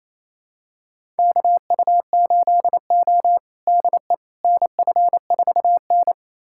Morse code, heard as a keyed tone, about 28 wpm, 700 Hz. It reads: KU8O DE NF4N
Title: USA Callsign DE Callsign - 4 Characters - 28wpm